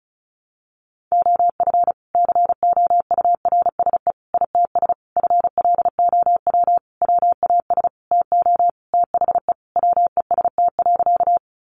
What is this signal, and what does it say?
Morse code, heard as a keyed tone, about 34 wpm, 700 Hz. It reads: OF COURSE ITS FLOW WAS TO THE WEST.
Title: Sentences from Top 1200 Words 35wpm